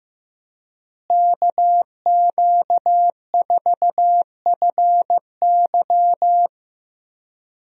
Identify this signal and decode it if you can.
Morse code, heard as a keyed tone, about 15 wpm, 700 Hz. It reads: KQ4FY
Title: USA Call Signs - 5 Characters - Repeated 2x - 15wpm